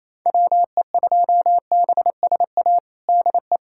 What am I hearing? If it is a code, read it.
Morse code, heard as a keyed tone, about 28 wpm, 700 Hz. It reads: WE2BSA DE
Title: USA Callsign DE Callsign - 6 Characters - 28wpm